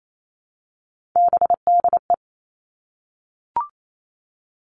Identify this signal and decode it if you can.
Morse code, heard as a keyed tone, about 28 wpm, 700 Hz. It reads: BDE
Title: Three Challenging Letter-Number Combinations 28wpm